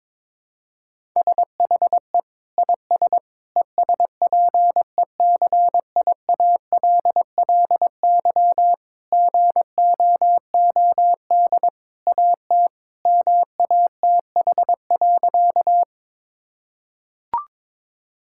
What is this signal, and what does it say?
Morse code, heard as a keyed tone, about 22 wpm, 700 Hz. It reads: SHE IS ESPECIALLY GOOD AT MATH.
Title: Sentences from Top 1000 Words - Review Part 6 - 22wpm